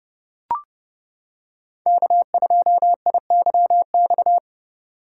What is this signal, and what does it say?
Morse code, heard as a keyed tone, about 29 wpm, 700 Hz. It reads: K2IYX